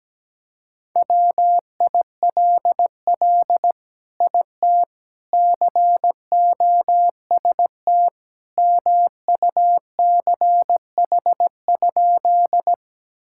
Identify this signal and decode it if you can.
Morse code, heard as a keyed tone, about 17 wpm, 700 Hz. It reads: WILL IT COST MUCH?
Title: Sentences from Top 1000 Words - Review Part 5 - 17wpm